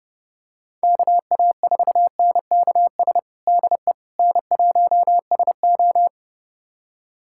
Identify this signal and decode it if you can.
Morse code, heard as a keyed tone, about 30 wpm, 700 Hz. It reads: KA4NKS DE N1SO
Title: USA Callsign DE Callsign 30wpm